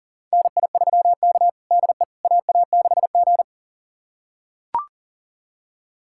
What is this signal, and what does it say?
Morse code, heard as a keyed tone, about 40 wpm, 700 Hz. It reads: NI3K DE AA6G
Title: USA Callsign DE Callsign - 4 Characters - 40wpm